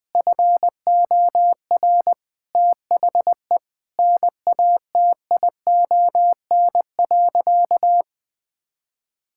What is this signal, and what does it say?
Morse code, heard as a keyed tone, about 21 wpm, 700 Hz. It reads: FOR THE NATION.
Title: Sentences with 4-Word Phrases - 20wpm